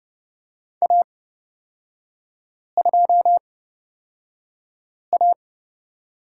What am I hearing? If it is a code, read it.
Morse code, heard as a keyed tone, about 30 wpm, spaced as for 5 wpm, 700 Hz. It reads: A2A
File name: Lesson 36e - BT - Three Characters 30wpm